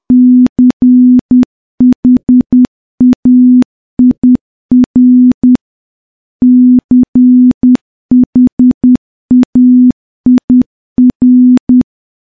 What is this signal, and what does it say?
Morse code, heard as a keyed tone, about 10 wpm, 250 Hz. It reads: CHAIR CHAIR
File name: morse-training-10WPM